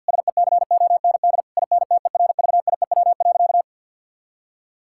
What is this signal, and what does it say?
Morse code, heard as a keyed tone, about 50 wpm, 700 Hz. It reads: SECOND INTERVIEW.
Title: Sentences from Top 1600 Words 50wpm